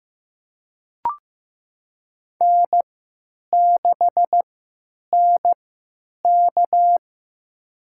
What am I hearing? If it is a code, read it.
Morse code, heard as a keyed tone, about 15 wpm, spaced as for 9 wpm, 700 Hz. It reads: N6NK
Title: USA Call Signs - 3x Character Spacing - 15wpm